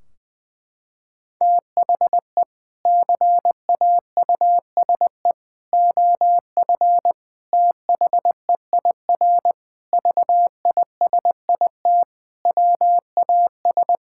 Morse code, heard as a keyed tone, about 21 wpm, 700 Hz. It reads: THE CAUSE OF THEIR VISIT WAS
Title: Sentences from Top 1600 Words 20wpm